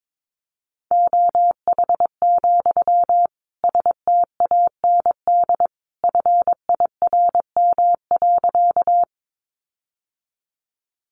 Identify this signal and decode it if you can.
Morse code, heard as a keyed tone, about 22 wpm, 700 Hz. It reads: OH, STAND FIRM.
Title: Sentences from Top 1400 Words 22wpm